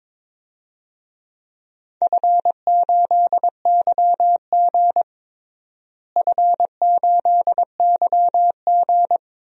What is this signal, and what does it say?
Morse code, heard as a keyed tone, about 22 wpm, 700 Hz. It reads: F8YG F8YG
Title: DXCC-100 Call Signs - 4 Characters - Repeated 2x - 22wpm